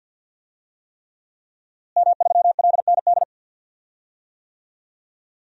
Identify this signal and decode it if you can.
Morse code, heard as a keyed tone, about 48 wpm, 700 Hz. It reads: M3LND
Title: DXCC-100 Call Signs - 5 Characters - Repeated 3x - 50wpm